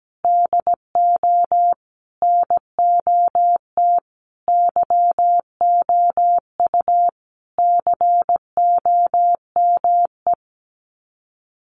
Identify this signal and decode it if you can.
Morse code, heard as a keyed tone, about 17 wpm, 700 Hz. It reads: DO NOT YOU COME
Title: Sets of 5 Words Encore2 17wpm